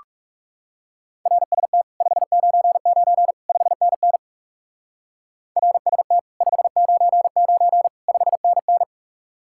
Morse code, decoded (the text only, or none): RST 599 5NN RST 599 5NN